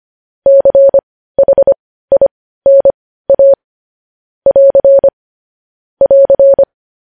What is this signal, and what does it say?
Morse code, heard as a keyed tone, about 25 wpm, spaced as for 15 wpm, 550 Hz. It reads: CHINA <AR> <AR>
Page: News Headlines in Morse Code at 15 WPM